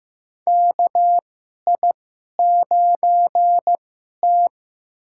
Morse code, heard as a keyed tone, about 15 wpm, spaced as for 11 wpm, 700 Hz. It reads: KI9T
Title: USA Call Signs - 4 Characters - 2x Character Spacing - 15wpm